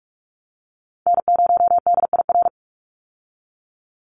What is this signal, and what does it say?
Morse code, heard as a keyed tone, about 44 wpm, 700 Hz. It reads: N0DIR
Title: Call Signs Repeated 2x - 45wpm